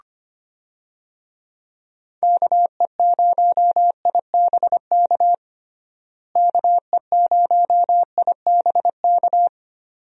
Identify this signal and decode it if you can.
Morse code, heard as a keyed tone, about 25 wpm, 700 Hz. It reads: KE0IBK KE0IBK
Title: USA Call Signs - 6 Characters - Repeated 2x - 25wpm